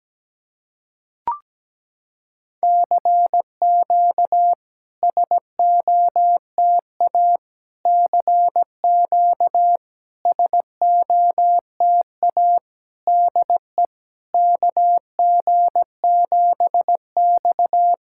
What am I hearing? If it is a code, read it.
Morse code, heard as a keyed tone, about 17 wpm, 700 Hz. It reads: CQ SOTA CQ SOTA DE KG7X
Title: SOTA QSO Practice 17wpm